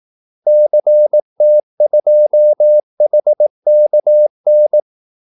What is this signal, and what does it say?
Morse code, heard as a keyed tone, about 18 wpm, 600 Hz. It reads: CT2HKN
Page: Kaldesignaler 18 wpm